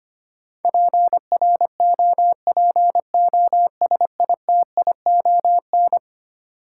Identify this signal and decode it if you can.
Morse code, heard as a keyed tone, about 25 wpm, 700 Hz. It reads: PROPOSITION
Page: morse-code-ninja-practice-25wpm